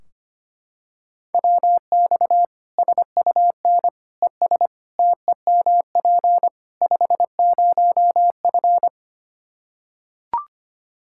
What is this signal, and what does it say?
Morse code, heard as a keyed tone, about 25 wpm, 700 Hz. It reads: WX SUN ES TEMP 50F
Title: Practice QSO - Exchange 2 - Info - 25wpm